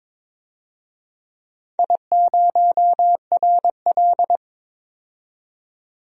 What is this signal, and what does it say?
Morse code, heard as a keyed tone, about 21 wpm, 700 Hz. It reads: I0RL
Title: DXCC-100 Call Signs - 4 Characters - 22wpm